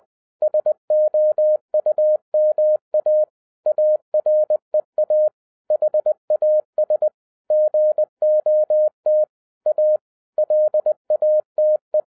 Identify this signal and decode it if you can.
Morse code, heard as a keyed tone, about 20 wpm, 610 Hz. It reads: SOUMA AREA HAS GOT A LATE
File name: 2011-10-mp3 / n03-TARO JR0QWW 20wpm